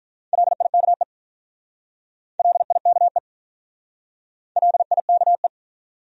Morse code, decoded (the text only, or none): LIKE LIKE LIKE